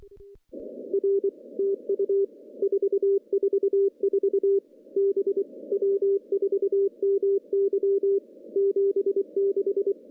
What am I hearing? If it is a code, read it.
Morse code, heard as a keyed tone, about 23 wpm, 410 Hz. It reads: R TU 444 B W4MY 76
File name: CW-SS-k5tr